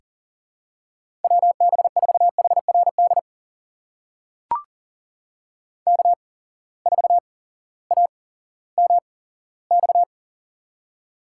Morse code, heard as a keyed tone, about 38 wpm, spaced as for 18 wpm, 700 Hz. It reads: WB4HRD K4AMX
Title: USA Call Signs - 8x Character Spacing - 40wpm